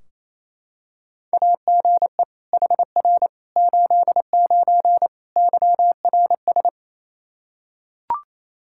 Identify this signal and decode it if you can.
Morse code, heard as a keyed tone, about 28 wpm, 700 Hz. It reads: AGE HR 89 YRS
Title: Practice QSO - Exchange 3 - More Info - 28wpm